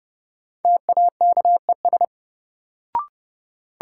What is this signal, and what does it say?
Morse code, heard as a keyed tone, about 30 wpm, 700 Hz. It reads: TAKES
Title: Sentences from Top 1000 Words - Review Part 6 - New Word Drill - 30wpm